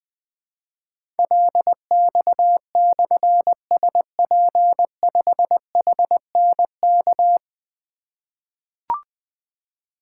Morse code, heard as a keyed tone, about 20 wpm, 700 Hz. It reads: LX/SP5HNK
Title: Difficult Call Signs 20wpm